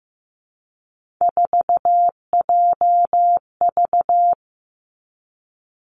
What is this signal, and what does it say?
Morse code, heard as a keyed tone, about 15 wpm, 700 Hz. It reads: 4JV
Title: Three Challenging Letter-Number Combinations 15wpm